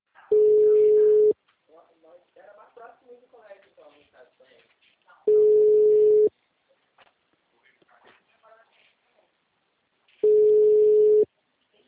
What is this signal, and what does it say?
Ringback tone (Europe, 425 Hz), with speech.
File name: out- -2007- - - .36036